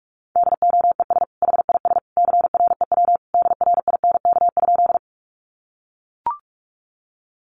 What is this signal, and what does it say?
Morse code, heard as a keyed tone, about 44 wpm, 700 Hz. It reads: DOES HIS CREW DRINK?
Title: Sentences from Top 1500 Words 45wpm